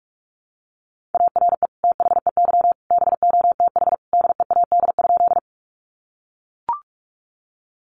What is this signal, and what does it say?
Morse code, heard as a keyed tone, about 44 wpm, 700 Hz. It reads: ARE THEY BOTH DEAD?
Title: Sentences from Top 1500 Words 45wpm